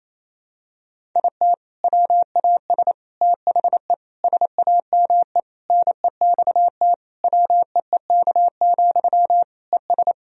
Morse code, heard as a keyed tone, about 28 wpm, 700 Hz. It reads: IT WAS THE SAME NEXT WEEK, ES
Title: Sentences from Top 500 Words and Abbreviations 28wpm